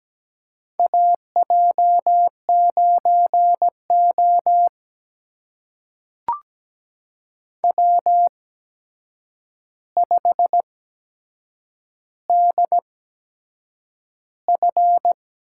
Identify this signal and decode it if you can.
Morse code, heard as a keyed tone, about 17 wpm, spaced as for 7 wpm, 700 Hz. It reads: AJ9O W5DF
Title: USA Call Signs - 8x Character Spacing - 17wpm